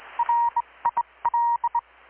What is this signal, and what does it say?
Morse code, heard as a keyed tone, about 20 wpm, spaced as for 18 wpm, 940 Hz, with radio noise. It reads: RIL
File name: blog / swl / 6-abril-2019-12695-USB-KPS